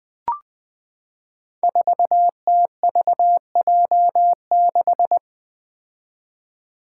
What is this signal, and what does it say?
Morse code, heard as a keyed tone, about 21 wpm, 700 Hz. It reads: 4TVJ6
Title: Five Challenging Letter-Number Combinations 20wpm